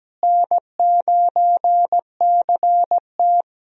Morse code, heard as a keyed tone, about 17 wpm, 700 Hz. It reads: N9CT
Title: USA Call Signs - 4 Characters - Repeated 3x - 17wpm